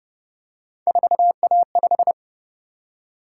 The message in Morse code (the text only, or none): 4A5